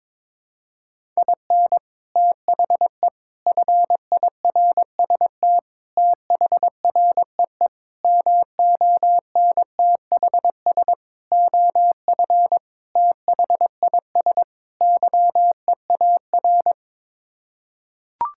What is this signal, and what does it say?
Morse code, heard as a keyed tone, about 22 wpm, 700 Hz. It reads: IN THE FIRST THREE MONTHS OF THIS YEAR
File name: practice / 22wpm / Sets of 8 Words - 22wpm